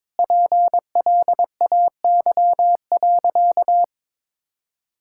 Morse code, heard as a keyed tone, about 22 wpm, 700 Hz. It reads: PLAY.
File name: practice / 22wpm / Sentences with 4-Word Phrases - 22wpm